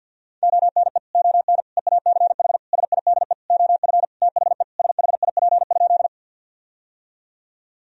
Morse code, heard as a keyed tone, about 51 wpm, 700 Hz. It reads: ONE ON EACH SIDE OF THE SHIP?
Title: Sentences from Top 1000 Words - Review Part 2 - 50wpm